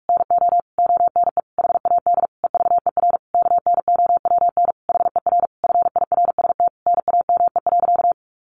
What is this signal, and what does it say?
Morse code, heard as a keyed tone, about 44 wpm, 700 Hz. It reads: NO ONE HAD EVER KNOWN HER FIRST NAME.
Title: Sentences from Top 1200 Words 45wpm